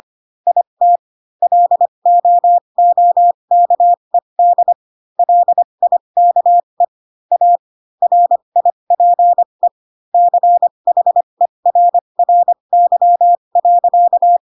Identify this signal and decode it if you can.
Morse code, heard as a keyed tone, about 25 wpm, spaced as for 22 wpm, 700 Hz. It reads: IT LOOKED LIKE A RIPE CHERRY.